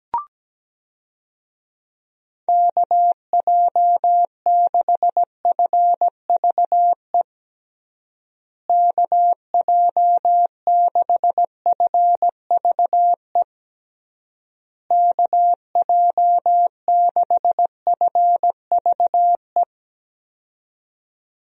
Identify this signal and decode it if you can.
Morse code, heard as a keyed tone, about 17 wpm, 700 Hz. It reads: KJ6FVE KJ6FVE KJ6FVE
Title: USA Call Signs - 6 Characters - Repeated 3x - 17wpm